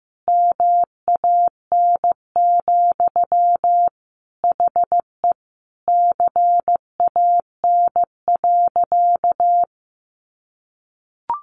Morse code, heard as a keyed tone, about 15 wpm, 700 Hz. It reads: MAN, HE CAN.
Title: Sentences from Top 1500 Words 15wpm